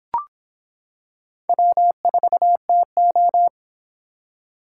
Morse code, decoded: W4TO

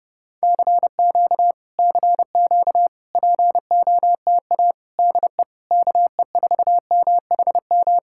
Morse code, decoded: CQ CQ POTA DE KE4MHM

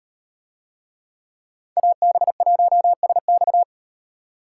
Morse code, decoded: AB1SX

